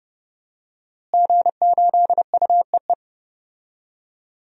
Morse code, heard as a keyed tone, about 30 wpm, 700 Hz. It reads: G8UEE
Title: DXCC-100 Call Signs - 5 Characters - Repeated 3x - 30wpm